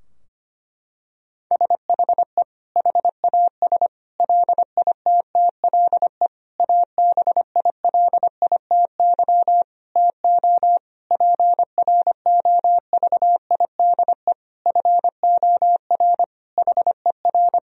Morse code, heard as a keyed tone, about 25 wpm, 700 Hz. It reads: SHE HAS LITTLE ABILITY TO PROVIDE FOR HER
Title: Sentences from Top 1000 Words - Review Part 3 - 25wpm